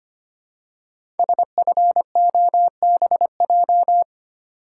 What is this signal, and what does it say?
Morse code, heard as a keyed tone, about 25 wpm, 700 Hz. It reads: SFOBJ
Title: Code Groups - 5 Letters - 25wpm